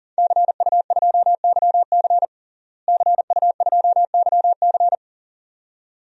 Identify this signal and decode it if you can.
Morse code, heard as a keyed tone, about 40 wpm, 700 Hz. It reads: CU2YC CU2YC
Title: DXCC-100 Call Signs - 5 Characters - Repeated 2x - 40wpm